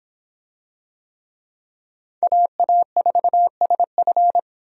Morse code, heard as a keyed tone, about 25 wpm, 700 Hz. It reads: AA4SF